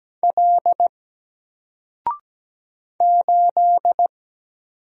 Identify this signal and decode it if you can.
Morse code, heard as a keyed tone, about 17 wpm, 700 Hz. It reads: L 8
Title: Single Letter-Number 17wpm